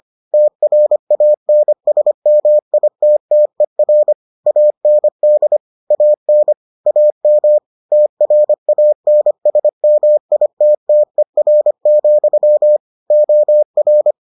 Morse code, heard as a keyed tone, about 25 wpm, 600 Hz. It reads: TRANSMITTER AND AN AM TRANSMITTER, OR